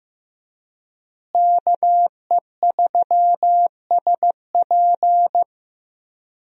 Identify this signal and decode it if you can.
Morse code, heard as a keyed tone, about 15 wpm, 700 Hz. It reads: KE3SP
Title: USA Call Signs - 5 Characters - Repeated 3x - 15wpm